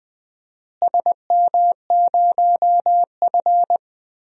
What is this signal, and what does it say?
Morse code, heard as a keyed tone, about 21 wpm, 700 Hz. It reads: SM0F
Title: DXCC-100 Call Signs - 4 Characters - Repeated 2x - 20wpm